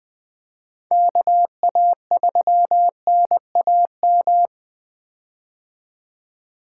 Morse code, heard as a keyed tone, about 21 wpm, 700 Hz. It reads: KA3NAM
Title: K1USN SST QSO Practice - 20wpm